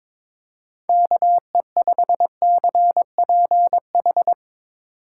Morse code, heard as a keyed tone, about 24 wpm, 700 Hz. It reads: KE5CPH
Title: Speed Racing - USA 2x3 Call Signs 22wpm